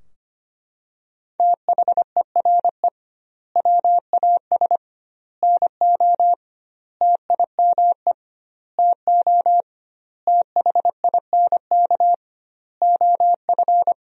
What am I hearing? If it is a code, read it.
Morse code, heard as a keyed tone, about 25 wpm, 700 Hz. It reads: THERE WAS NO TIME TO THINK OF
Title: 2x Word-Spacing - Sentences from Top 400 Words - 25wpm